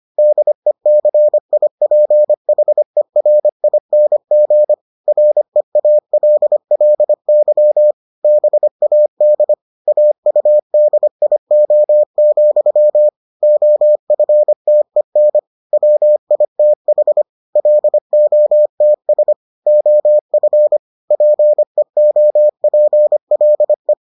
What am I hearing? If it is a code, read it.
Morse code, heard as a keyed tone, about 25 wpm, 600 Hz. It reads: DECIPHERING REALLY BAD AUDIO, OFTEN WITH LOTS OF PEOPLE